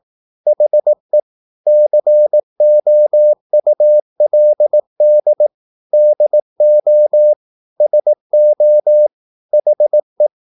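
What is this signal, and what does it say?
Morse code, heard as a keyed tone, about 18 wpm, 600 Hz. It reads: HE COULD DO SO HE
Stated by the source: Morse practice at 18 wpm, English text, from CW med Gnister.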